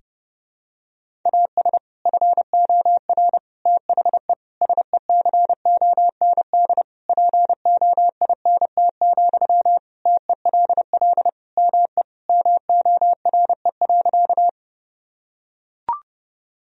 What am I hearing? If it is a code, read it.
Morse code, heard as a keyed tone, about 30 wpm, 700 Hz. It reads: AS FOR THE SECOND POINT, TELL ME MORE.
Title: Sentences from Top 1000 Words - Review Part 1 - 30wpm